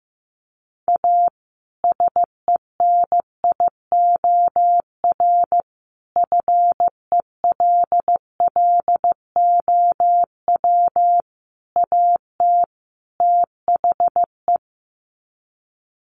Morse code, heard as a keyed tone, about 15 wpm, 700 Hz. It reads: A SENIOR FELLOW AT THE
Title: Sets of 5 Words Encore 15wpm